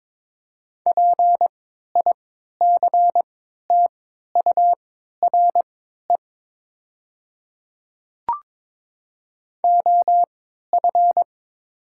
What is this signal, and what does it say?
Morse code, heard as a keyed tone, about 22 wpm, spaced as for 13 wpm, 700 Hz. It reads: PICTURE OF
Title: 3x Character Spacing - Top 200 Words - 22wpm